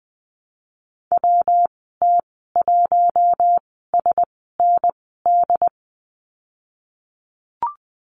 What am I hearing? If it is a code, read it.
Morse code, heard as a keyed tone, about 20 wpm, spaced as for 15 wpm, 700 Hz. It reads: WT1SND
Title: 128 Lesson 21g - Number 6 - Call Signs 20wpm